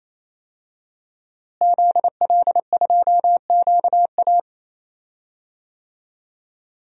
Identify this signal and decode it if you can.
Morse code, heard as a keyed tone, about 28 wpm, 700 Hz. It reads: ZL2QA
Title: DXCC-100 Call Signs - 5 Characters - Repeated 3x - 28wpm